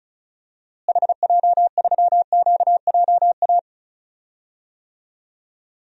Morse code, heard as a keyed tone, about 36 wpm, 700 Hz. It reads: HJ3QJA
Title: DXCC-100 Call Signs - 6 Characters - Repeated 2x - 35wpm